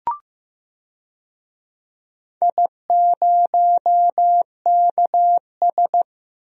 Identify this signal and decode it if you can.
Morse code, heard as a keyed tone, about 15 wpm, 700 Hz. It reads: I0KS
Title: DXCC-100 Call Signs - 4 Characters - 15wpm